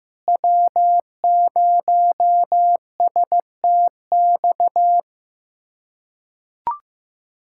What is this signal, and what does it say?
Morse code, heard as a keyed tone, about 16 wpm, 700 Hz. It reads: W0STX
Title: USA Callsign DE Callsign - 5 Characters - 15wpm